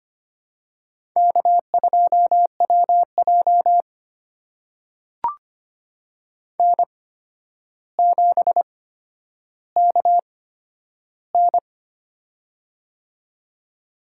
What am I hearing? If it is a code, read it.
Morse code, heard as a keyed tone, about 25 wpm, spaced as for 11 wpm, 700 Hz. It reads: K2WJ N7KN
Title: USA Call Signs - 4 Characters - 8x Character Spacing - 25wpm